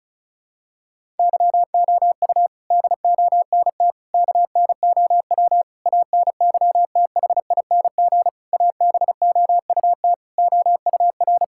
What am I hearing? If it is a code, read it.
Morse code, heard as a keyed tone, about 34 wpm, 700 Hz. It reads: YOU DONT KNOW ANYTHING ABOUT OUR
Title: Sentences from Top 1000 Words - Review Part 6 - 35wpm